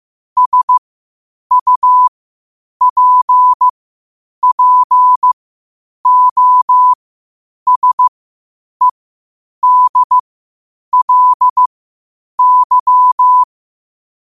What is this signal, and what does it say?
Morse code, heard as a keyed tone, about 15 wpm, spaced as for 8 wpm, 1000 Hz. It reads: SUPPOSEDLY
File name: qotdfiles / 8wpm